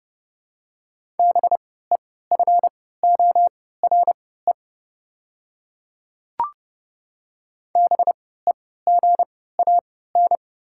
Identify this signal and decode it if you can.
Morse code, heard as a keyed tone, about 30 wpm, spaced as for 17 wpm, 700 Hz. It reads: BEFORE BEGAN
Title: 3x Character Spacing - Top 300 Words - 30wpm